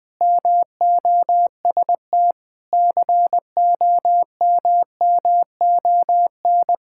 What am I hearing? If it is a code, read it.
Morse code, heard as a keyed tone, about 19 wpm, 700 Hz. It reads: MOST COMMON
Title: Sentences with 4-Word Phrases - 20wpm